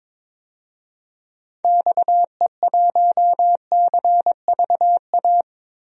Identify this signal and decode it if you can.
Morse code, heard as a keyed tone, about 22 wpm, 700 Hz. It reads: XE1CVA
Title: DXCC-100 Call Signs - 6 Characters - 22wpm